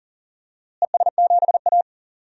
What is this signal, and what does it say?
Morse code, heard as a keyed tone, about 40 wpm, 700 Hz. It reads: ES7A
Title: DXCC-100 Call Signs - 4 Characters - Repeated 3x - 40wpm